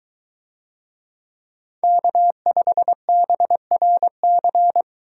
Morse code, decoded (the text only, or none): K5BRC